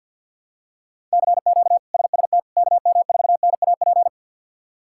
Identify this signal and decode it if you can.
Morse code, heard as a keyed tone, about 51 wpm, 700 Hz. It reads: CQ SST KM4NAP
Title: K1USN SST QSO Practice - 50wpm